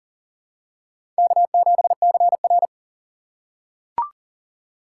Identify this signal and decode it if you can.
Morse code, heard as a keyed tone, about 40 wpm, 700 Hz. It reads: K7CR
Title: USA Call Signs - 4 Characters - 4x Character Spacing - 40wpm